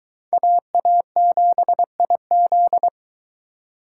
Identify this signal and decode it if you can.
Morse code, heard as a keyed tone, about 23 wpm, 700 Hz. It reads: AA7IZ